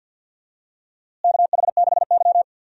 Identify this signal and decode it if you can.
Morse code, heard as a keyed tone, about 51 wpm, 700 Hz. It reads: KH6Y
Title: DXCC-100 Call Signs - 4 Characters - Repeated 2x - 50wpm